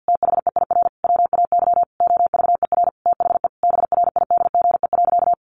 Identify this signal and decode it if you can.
Morse code, heard as a keyed tone, about 50 wpm, 700 Hz. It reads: THEIR WAY OVER THE BRIDGE.
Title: Sentences from Top 1500 Words 50wpm